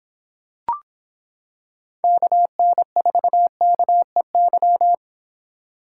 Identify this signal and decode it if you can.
Morse code, heard as a keyed tone, about 25 wpm, 700 Hz. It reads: KN4KEY